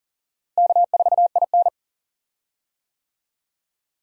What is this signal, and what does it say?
Morse code, heard as a keyed tone, about 40 wpm, 700 Hz. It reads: K4IN
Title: USA Call Signs - 4 Characters - Repeated 3x - 40wpm